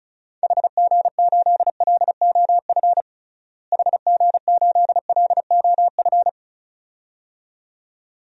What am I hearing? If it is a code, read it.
Morse code, heard as a keyed tone, about 36 wpm, 700 Hz. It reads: HG8LOF HG8LOF
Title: DXCC-100 Call Signs - 6 Characters - Repeated 2x - 35wpm